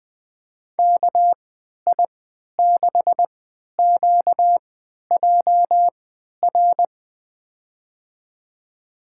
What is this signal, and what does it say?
Morse code, heard as a keyed tone, about 20 wpm, spaced as for 11 wpm, 700 Hz. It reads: KI6QJR
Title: USA Call Signs - 6 Characters - 3x Character Spacing - 20wpm